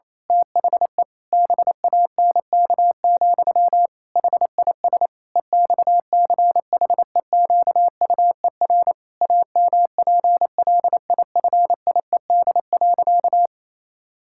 Morse code, heard as a keyed tone, about 28 wpm, 700 Hz. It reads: THE BANK, HIS EXCHEQUER AMPLIFIED.
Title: Pangram Challenge v2 - 28wpm